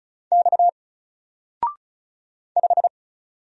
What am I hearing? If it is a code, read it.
Morse code, heard as a keyed tone, about 35 wpm, 700 Hz. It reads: X 5
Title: Single Character (Letter:Number:Punctuation:Prosign) 35wpm